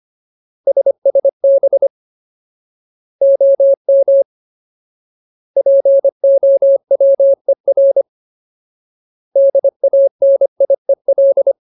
Morse code, heard as a keyed tone, about 25 wpm, 550 Hz. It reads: SSB OM POWER DANIEL
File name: Ultimate Word List / 25WPM Ultimate